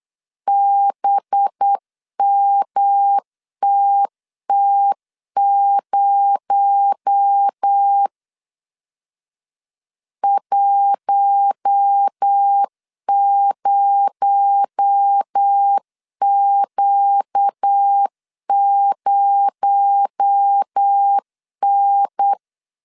Morse code, heard as a keyed tone, about 8 wpm, 800 Hz. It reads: BMTT0 10Q0N
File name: 8wpm